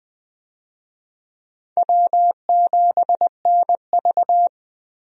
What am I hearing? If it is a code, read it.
Morse code, heard as a keyed tone, about 20 wpm, 700 Hz. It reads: W7NV
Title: USA Call Signs - 4 Characters - 20wpm